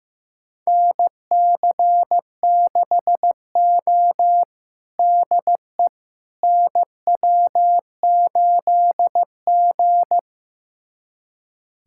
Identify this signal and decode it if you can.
Morse code, heard as a keyed tone, about 15 wpm, 700 Hz. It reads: NC6O DE NW8G
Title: USA Callsign DE Callsign - 4 Characters - 15wpm